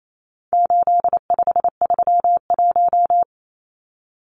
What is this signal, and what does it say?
Morse code, heard as a keyed tone, about 28 wpm, 700 Hz. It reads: 8531
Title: Numbers - 4-Digit 28wpm